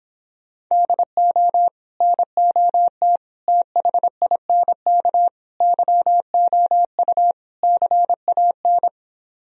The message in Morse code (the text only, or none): DO NOT THINK YOU CAN